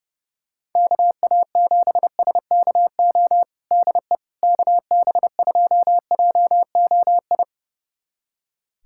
Morse code, heard as a keyed tone, about 30 wpm, 700 Hz. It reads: KA7SKO DE KB2JOI
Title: USA Callsign DE Callsign - 6 Characters - 30wpm